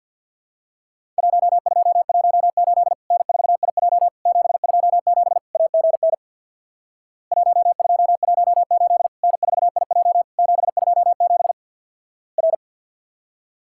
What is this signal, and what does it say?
Morse code, heard as a keyed tone, about 50 wpm, 700 Hz. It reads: 1218 N4IJ 727 AGN 1218 N4IJ 727 R
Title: Worked All Europe DX Contest - QTC Exchange with Repeats - 50wpm